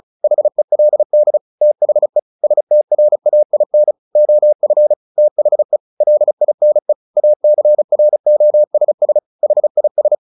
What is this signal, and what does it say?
Morse code, heard as a keyed tone, about 34 wpm, 600 Hz. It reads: HELD THE STRAIN OF THE LINE ACROSS HIS
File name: the old man and the sea 2 35wpm